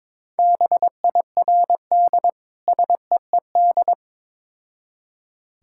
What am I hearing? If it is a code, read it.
Morse code, heard as a keyed tone, about 22 wpm, 700 Hz. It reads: BIRD SEED
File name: practice / 22wpm / Difficult Words with Hints - 22wpm